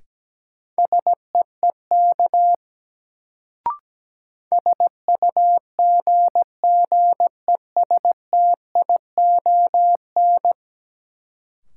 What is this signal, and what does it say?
Morse code, heard as a keyed tone, about 17 wpm, 700 Hz. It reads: SEEK SUGGESTION
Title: Sentences from Top 1600 Words - New Word Drill Alt - 17wpm